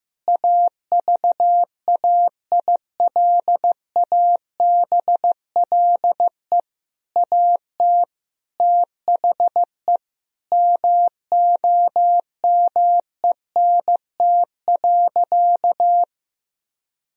Morse code, heard as a keyed tone, about 15 wpm, 700 Hz. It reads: AVAILABLE AT THE MOMENT.
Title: Sentences from Top 1000 Words - Review Part 3 - 15wpm